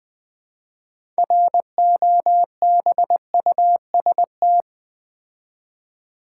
Morse code, heard as a keyed tone, about 20 wpm, 700 Hz. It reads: ROBUST
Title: Build A Word - 3 Letters At A Time - 6-Letter Words - 20wpm